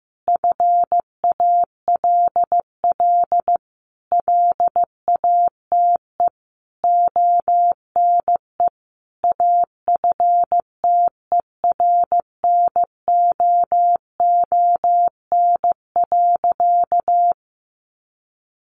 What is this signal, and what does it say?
Morse code, heard as a keyed tone, about 15 wpm, 700 Hz. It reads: FALL LATE ONE AFTERNOON.
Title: Sentences from Top 1500 Words 15wpm